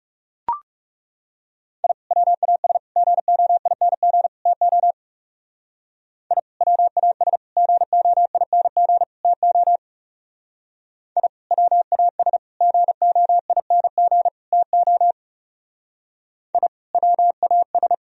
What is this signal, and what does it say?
Morse code, heard as a keyed tone, about 37 wpm, 700 Hz. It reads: I WAS GOING TO I WAS GOING TO I WAS GOING TO I WAS
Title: Speed-Racing - Sets of 4 Words 30wpm